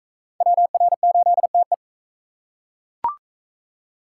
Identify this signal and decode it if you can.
Morse code, heard as a keyed tone, about 36 wpm, 700 Hz. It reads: WR8TE